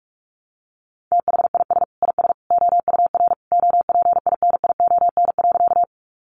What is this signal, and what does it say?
Morse code, heard as a keyed tone, about 44 wpm, 700 Hz. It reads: THIS IS OUR OPINION.
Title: Sentences from Top 1200 Words 45wpm